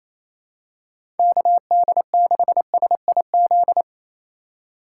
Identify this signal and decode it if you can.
Morse code, heard as a keyed tone, about 28 wpm, 700 Hz. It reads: KD6SIZ
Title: USA Call Signs - 6 Characters - 2x Character Spacing - 28wpm